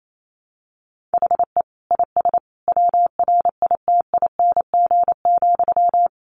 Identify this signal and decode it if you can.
Morse code, heard as a keyed tone, about 28 wpm, 700 Hz. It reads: HE IS WRITING,
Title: Sentences from Top 800 Words 28wpm